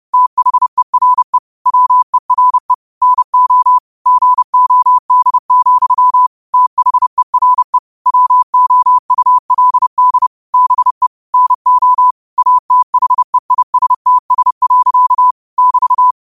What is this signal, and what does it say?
Morse code, heard as a keyed tone, about 30 wpm, 1000 Hz. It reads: THERE WERE NO GOD, THERE WOULD BE NO ATHEISTS. <BT>
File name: qotdfiles / 30wpm